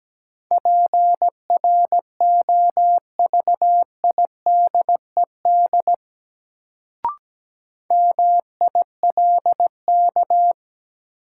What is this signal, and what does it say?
Morse code, heard as a keyed tone, about 17 wpm, 700 Hz. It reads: PROVIDED MILK
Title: Sentences from Top 1500 Words - New Word Drill - 17wpm